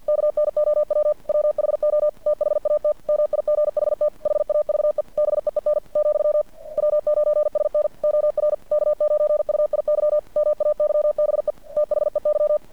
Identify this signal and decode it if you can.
Morse code, heard as a keyed tone, about 50 wpm, 600 Hz. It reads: KNOW WHO THAT MIGHT HAVE BEEN , W9SN OR K9UIY MAYBE THEY
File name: archived / CRSnet